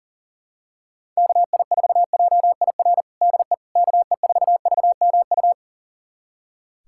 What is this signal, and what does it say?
Morse code, heard as a keyed tone, about 40 wpm, 700 Hz. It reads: KI4JIR DE KE4VMU
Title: USA Callsign DE Callsign - 6 Characters - 40wpm